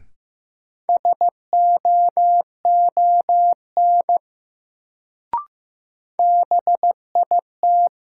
Morse code, heard as a keyed tone, about 15 wpm, 700 Hz. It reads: SOON BIT
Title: Sets of 3 Words - New Word Drill - 15wpm